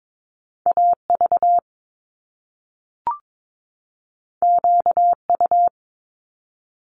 Morse code, heard as a keyed tone, about 22 wpm, 700 Hz. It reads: AV QU
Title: Top Two Letter Combinations in Words 22wpm